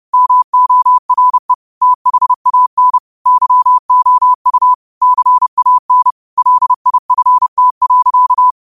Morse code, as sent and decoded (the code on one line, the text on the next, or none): -- --- .-. / - .... .- -. / -.-- --- ..- / -.-. .- -. / .-.. .. ..-. - .-.-.-
MORE THAN YOU CAN LIFT.